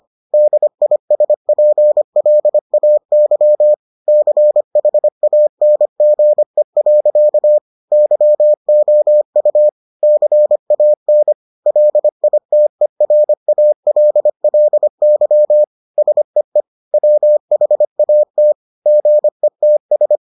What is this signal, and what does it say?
Morse code, heard as a keyed tone, about 25 wpm, 600 Hz. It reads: DISPLAY CHANGE. YOU CAN LITERALLY SEE WHAT GETS